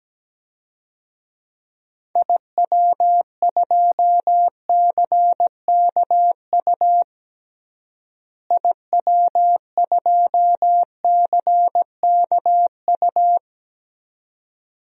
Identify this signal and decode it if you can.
Morse code, heard as a keyed tone, about 17 wpm, 700 Hz. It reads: IW2CKU IW2CKU
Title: DXCC-100 Call Signs - 6 Characters - Repeated 2x - 17wpm